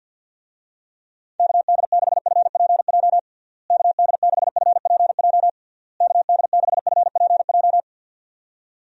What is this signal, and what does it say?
Morse code, heard as a keyed tone, about 51 wpm, 700 Hz. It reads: KD6FPJ KD6FPJ KD6FPJ
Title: USA Call Signs - 6 Characters - Repeated 3x - 50wpm